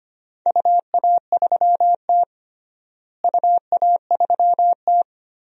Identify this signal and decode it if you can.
Morse code, heard as a keyed tone, about 25 wpm, 700 Hz. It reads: UA3T UA3T
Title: DXCC-100 Call Signs - 4 Characters - Repeated 3x - 25wpm